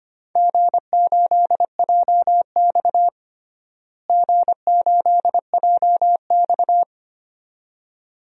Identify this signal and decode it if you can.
Morse code, heard as a keyed tone, about 25 wpm, 700 Hz. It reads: G8JX G8JX
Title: DXCC-100 Call Signs - 4 Characters - Repeated 2x - 25wpm